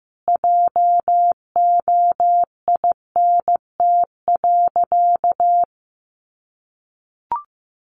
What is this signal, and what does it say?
Morse code, heard as a keyed tone, about 15 wpm, 700 Hz. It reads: JOINT.
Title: Sentences from Top 1300 Words 15wpm